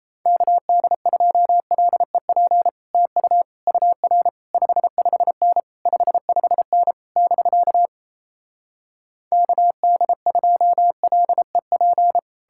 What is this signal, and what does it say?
Morse code, heard as a keyed tone, about 38 wpm, 700 Hz. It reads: KD2LEP TU UR 55N 55N <BK> KD2LEP